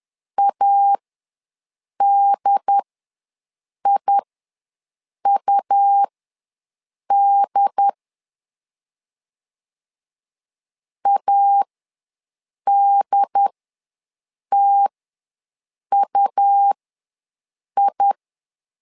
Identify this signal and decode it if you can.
Morse code, heard as a keyed tone, about 10 wpm, spaced as for 6 wpm, 800 Hz. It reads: ADIUD ADTUI
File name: LEZIONI / 2 UHDN / 2 UHD ws 10wpm 2058